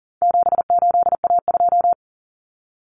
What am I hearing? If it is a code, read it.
Morse code, heard as a keyed tone, about 39 wpm, 700 Hz. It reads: 78A2
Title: Common Cut Numbers - 4-Digit 40wpm